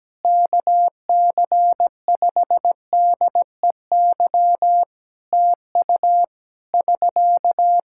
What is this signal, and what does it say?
Morse code, heard as a keyed tone, about 17 wpm, 700 Hz. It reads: KC5DEY TU <SK>
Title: Practice QSO - Exchange 4 - Ending - 17wpm